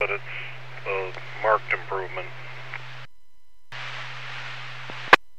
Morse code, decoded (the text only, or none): none